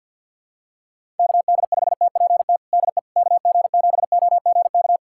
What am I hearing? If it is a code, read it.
Morse code, heard as a keyed tone, about 49 wpm, 700 Hz. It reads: KD5TPT DE KG7OGK
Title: USA Callsign DE Callsign 50wpm